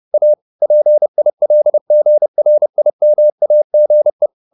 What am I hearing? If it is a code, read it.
Morse code, heard as a keyed tone, about 30 wpm, 600 Hz. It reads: A PILGRIMAGE
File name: the old man and the sea / the old man and the sea 1 30wpm